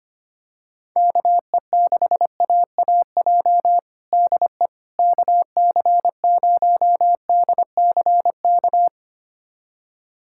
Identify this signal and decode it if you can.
Morse code, heard as a keyed tone, about 25 wpm, 700 Hz. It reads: KE6AAJ DE KC0DCK
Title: USA Callsign DE Callsign - 6 Characters - 25wpm